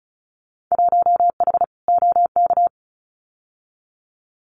Morse code, heard as a keyed tone, about 34 wpm, 700 Hz. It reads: 1H OK
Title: Winter Field Day Exchanges 35wpm